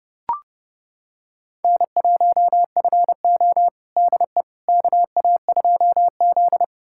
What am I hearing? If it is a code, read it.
Morse code, heard as a keyed tone, about 30 wpm, 700 Hz. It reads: N1FO DE KA2Z
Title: USA Callsign DE Callsign - 4 Characters - 30wpm